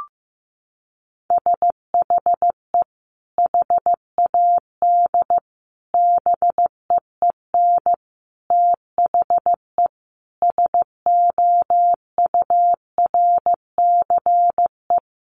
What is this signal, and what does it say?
Morse code, heard as a keyed tone, about 15 wpm, 700 Hz. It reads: SHE HAD BEEN THE SOURCE
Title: Sentences from Top 1200 Words 15wpm